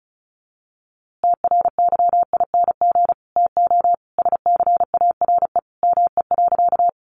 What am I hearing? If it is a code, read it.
Morse code, heard as a keyed tone, about 34 wpm, 700 Hz. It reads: TRYING TO SCARE ME.
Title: A Sound Of Thunder 35wpm